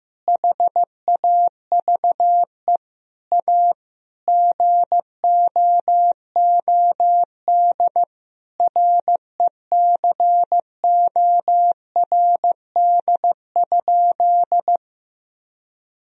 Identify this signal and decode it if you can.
Morse code, heard as a keyed tone, about 15 wpm, 700 Hz. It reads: HAVE A GOOD RECORD?
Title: Sentences from Top 1000 Words - Review Part 3 - 15wpm